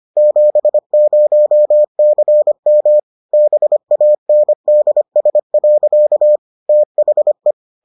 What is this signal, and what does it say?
Morse code, heard as a keyed tone, about 25 wpm, 600 Hz. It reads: 70CM BANDS. THE